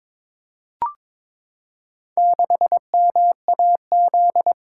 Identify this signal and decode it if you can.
Morse code, heard as a keyed tone, about 22 wpm, 700 Hz. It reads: 6MAZ
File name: practice / 22wpm / Four Letter-Number Combinations - 22wpm